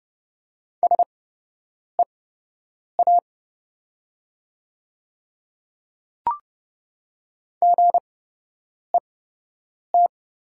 Morse code, heard as a keyed tone, about 30 wpm, spaced as for 8 wpm, 700 Hz. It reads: SEA GET
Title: 8x Character Spacing - Top 300 Words - 30wpm